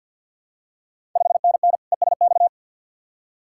Morse code, heard as a keyed tone, about 48 wpm, 690 Hz. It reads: 5NN ESX
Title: Ontario QSO Party Exchange 50wpm